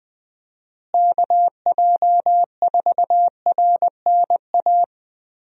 Morse code, heard as a keyed tone, about 20 wpm, 700 Hz. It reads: KJ4RNA
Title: USA Call Signs - 6 Characters - Repeated 2x - 20wpm